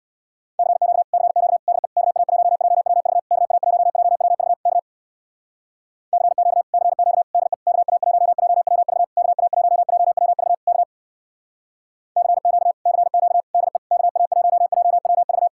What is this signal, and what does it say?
Morse code, heard as a keyed tone, about 62 wpm, 700 Hz. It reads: CQ CQ DE KA1JWV KA1JWV K CQ CQ DE KA1JWV KA1JWV K CQ CQ DE KA1JWV
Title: Speed-Racing Common QSO Phrases - 50wpm